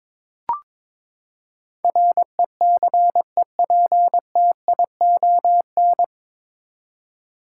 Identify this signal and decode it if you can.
Morse code, heard as a keyed tone, about 22 wpm, 700 Hz. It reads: RECEPTION
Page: Morse Code Ninja practice 22wpm